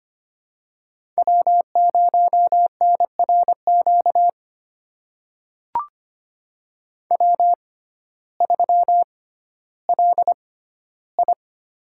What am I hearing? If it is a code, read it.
Morse code, heard as a keyed tone, about 25 wpm, spaced as for 14 wpm, 700 Hz. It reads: W0NRQ W3LI
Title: USA Call Signs - 6x Character Spacing - 25wpm